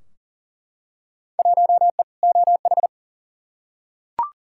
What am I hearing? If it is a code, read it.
Morse code, heard as a keyed tone, about 40 wpm, 700 Hz. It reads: 1E OH
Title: Field Day Exchanges - 40wpm